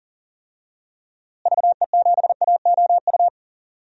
Morse code, heard as a keyed tone, about 40 wpm, 700 Hz. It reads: VE7AOU
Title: DXCC-100 Call Signs - 6 Characters - Repeated 2x - 40wpm